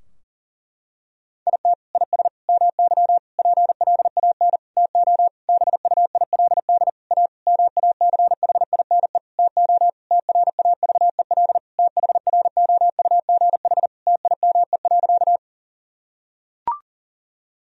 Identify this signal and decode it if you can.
Morse code, heard as a keyed tone, about 40 wpm, 700 Hz. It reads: IT IS MY PLAN TO BUILD A MACHINE TO TRAVEL THROUGH TIME.
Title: Sentences from Top 1000 Words - Review Part 2 - 40wpm